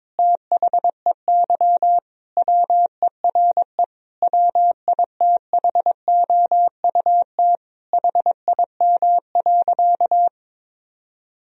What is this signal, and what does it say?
Morse code, heard as a keyed tone, about 22 wpm, 700 Hz. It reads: THEY WERE WITHOUT HIM.
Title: Sentences from Top 1000 Words - Review Part 3 - 22wpm